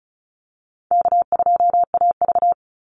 Morse code, heard as a keyed tone, about 34 wpm, 700 Hz. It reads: K2AV
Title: Call Signs - 35wpm